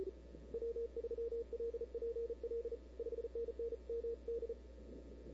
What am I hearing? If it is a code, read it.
Morse code, heard as a keyed tone, about 36 wpm, 460 Hz. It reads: W3LPL 5NN MD
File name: CW / 2005-k5tr arrl cw dx -00:00